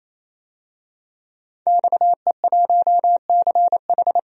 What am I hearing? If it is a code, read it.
Morse code, heard as a keyed tone, about 28 wpm, 700 Hz. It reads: XE1CH
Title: DXCC-100 Call Signs - 5 Characters - 28wpm